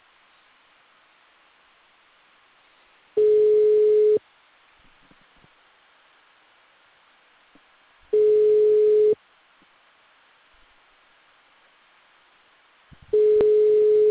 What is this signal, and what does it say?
Ringback tone (Europe, 425 Hz).
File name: out- -2009- - - .64